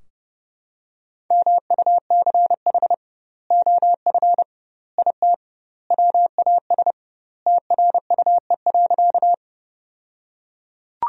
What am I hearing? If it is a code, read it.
Morse code, heard as a keyed tone, about 30 wpm, 700 Hz. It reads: MUCH OF IT WAS TRUE.
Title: 2x Word-Spacing - Sentences from Top 400 Words - 30wpm